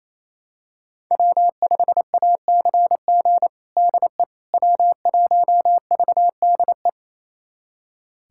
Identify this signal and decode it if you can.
Morse code, heard as a keyed tone, about 27 wpm, 700 Hz. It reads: W5ACG DE W1VDE
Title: USA Callsign DE Callsign - 5 Characters - 28wpm